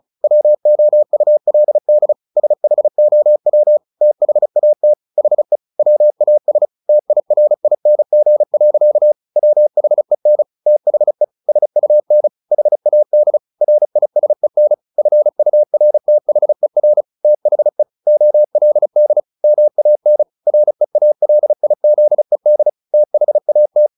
Morse code, heard as a keyed tone, about 34 wpm, 600 Hz. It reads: WOULD SHOW THAT HE WAS TIRING. WHEN THE SUN HAD RISEN FURTHER THE OLD MAN REALIZED THAT
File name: the old man and the sea / the old man and the sea 1 35wpm